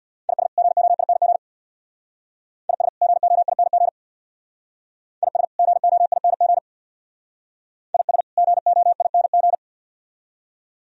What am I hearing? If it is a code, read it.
Morse code, heard as a keyed tone, about 58 wpm, 700 Hz. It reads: IS GOING IS GOING IS GOING IS GOING
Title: Speed-Racing - Sets of 2 Words 50wpm